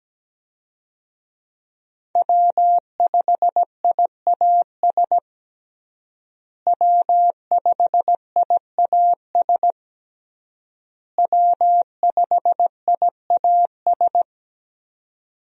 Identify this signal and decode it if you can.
Morse code, heard as a keyed tone, about 17 wpm, 700 Hz. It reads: W5IAS W5IAS W5IAS
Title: USA Call Signs - 5 Characters - Repeated 3x - 17wpm